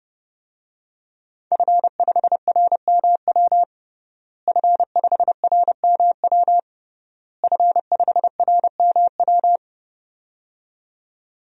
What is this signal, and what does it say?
Morse code, heard as a keyed tone, about 30 wpm, 700 Hz. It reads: F5RMW F5RMW F5RMW
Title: DXCC-100 Call Signs - 5 Characters - Repeated 3x - 30wpm